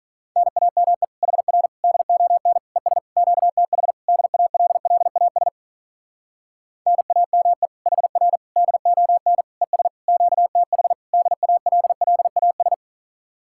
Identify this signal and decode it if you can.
Morse code, heard as a keyed tone, about 46 wpm, 700 Hz. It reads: NAME HR DON ES QTH DALLAS NAME HR DON ES QTH DALLAS